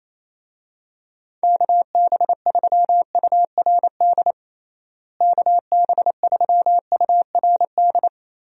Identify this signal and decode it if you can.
Morse code, heard as a keyed tone, about 28 wpm, 700 Hz. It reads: KB3URD KB3URD
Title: USA Call Signs - 6 Characters - Repeated 3x - 28wpm